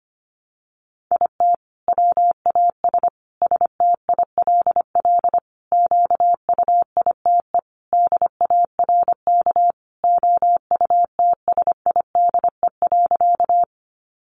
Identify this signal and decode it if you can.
Morse code, heard as a keyed tone, about 25 wpm, 700 Hz. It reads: IT WAS STILL QUITE DARK OUTSIDE.
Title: Sentences from Top 900 Words 25wpm